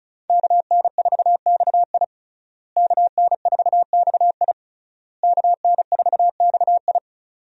Morse code, heard as a keyed tone, about 36 wpm, 700 Hz. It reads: KN4XI KN4XI KN4XI
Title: USA Call Signs - 5 Characters - Repeated 3x - 35wpm